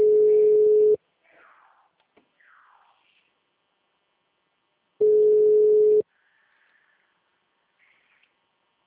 Ringback tone (Europe, 425 Hz).